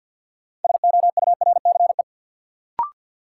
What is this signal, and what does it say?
Morse code, heard as a keyed tone, about 46 wpm, 700 Hz. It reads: SOURCE
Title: Sentences from Top 800 Words - New Word Drill - 50wpm